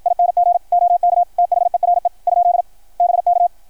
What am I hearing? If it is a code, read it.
Morse code, heard as a keyed tone, about 53 wpm, 700 Hz. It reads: ING OK THERE ? BK
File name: archived / CRSnet